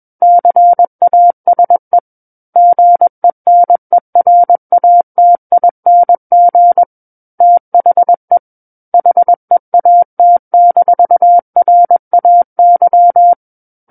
Morse code, heard as a keyed tone, about 21 wpm, 700 Hz. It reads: CASE GENERATING THE HEAT-RAY
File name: Book One / 12-Chapter 12 - 21 WPM